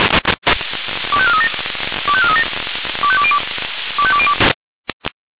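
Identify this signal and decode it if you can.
Five-tone selective call (ZVEI), heard as quick runs of five tones, tones of about 70 ms. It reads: 25437 25437 25202 25202
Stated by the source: ZVEI five-tone call 25431 received on an SDR.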